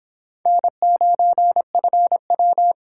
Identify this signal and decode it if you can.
Morse code, heard as a keyed tone, about 25 wpm, 700 Hz. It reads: N9FW